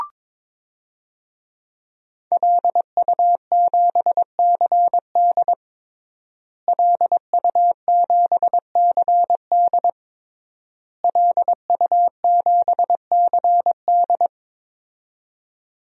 Morse code, heard as a keyed tone, about 22 wpm, 700 Hz. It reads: LU7CD LU7CD LU7CD
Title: DXCC-100 Call Signs - 5 Characters - Repeated 3x - 22wpm